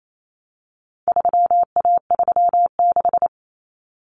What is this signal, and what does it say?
Morse code, heard as a keyed tone, about 28 wpm, 700 Hz. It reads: 3A36
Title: Common Cut Numbers - 4-Digit 28wpm